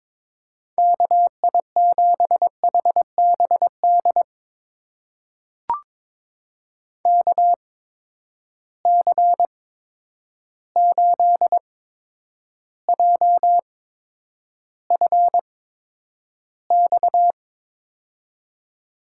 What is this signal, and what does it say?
Morse code, heard as a keyed tone, about 22 wpm, spaced as for 9 wpm, 700 Hz. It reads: KI7HBD KC8JFX
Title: USA Call Signs - 6 Characters - 8x Character Spacing - 22wpm